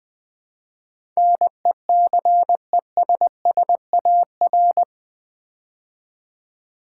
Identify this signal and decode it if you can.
Morse code, heard as a keyed tone, about 20 wpm, 700 Hz. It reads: NECESSAR
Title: Build A Word - 4 Letters At A Time - 11-Letter Words - 20wpm